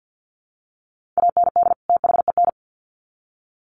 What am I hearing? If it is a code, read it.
Morse code, heard as a keyed tone, about 50 wpm, 700 Hz. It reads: AND THEN
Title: Sets of 2 Words 50wpm